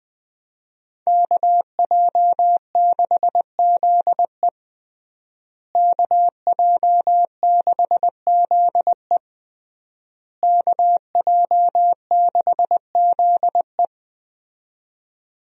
Morse code, heard as a keyed tone, about 20 wpm, 700 Hz. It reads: KJ6ZE KJ6ZE KJ6ZE
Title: USA Call Signs - 5 Characters - Repeated 3x - 20wpm